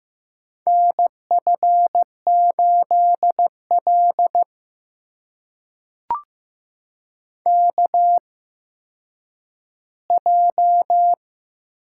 Morse code, heard as a keyed tone, about 15 wpm, spaced as for 9 wpm, 700 Hz. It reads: NF8L KJ
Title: USA Call Signs - 4 Characters - 8x Character Spacing - 15wpm